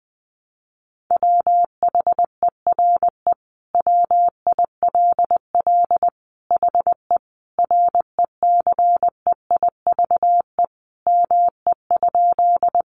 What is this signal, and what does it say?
Morse code, heard as a keyed tone, about 20 wpm, 700 Hz. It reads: WHERE WILL HE RECEIVE ME?
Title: Sentences from Top 1000 Words 20wpm